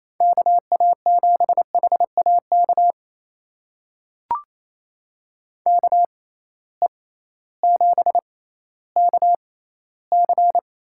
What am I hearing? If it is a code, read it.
Morse code, heard as a keyed tone, about 28 wpm, spaced as for 15 wpm, 700 Hz. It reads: KA7HAK KE7KC
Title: USA Call Signs - 6 Characters - 6x Character Spacing - 28wpm